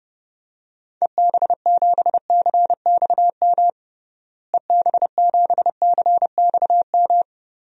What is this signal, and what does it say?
Morse code, heard as a keyed tone, about 32 wpm, 700 Hz. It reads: EB7CXM EB7CXM
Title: DXCC-100 Call Signs - 6 Characters - Repeated 2x - 30wpm